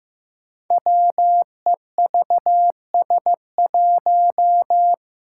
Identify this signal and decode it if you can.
Morse code, heard as a keyed tone, about 15 wpm, 700 Hz. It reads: WEVS1
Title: Five Challenging Letter-Number Combinations 15wpm